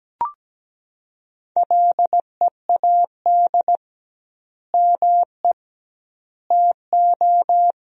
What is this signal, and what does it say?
Morse code, heard as a keyed tone, about 17 wpm, 700 Hz. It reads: LEAD ME TO
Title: 2x Word-Spacing - Sentences from Top 500 Words - 17wpm